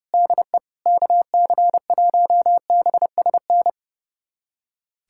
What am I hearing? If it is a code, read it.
Morse code, heard as a keyed tone, about 30 wpm, 700 Hz. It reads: DE KC1BSN
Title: USA Callsign DE Callsign 30wpm